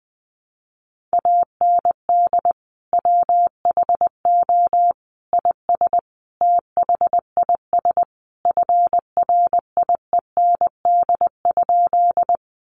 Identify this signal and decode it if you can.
Morse code, heard as a keyed tone, about 20 wpm, 700 Hz. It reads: AND WHO IS THIS FRIEND?
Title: Sentences from Top 300 Words 20wpm